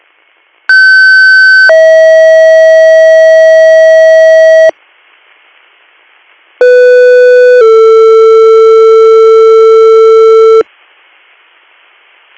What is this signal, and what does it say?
Two-tone page, heard as a one-second tone, then a three-second tone, about 1550 then 640 Hz, about 500 then 450 Hz. It is heard from 0.7 to 10.6 s.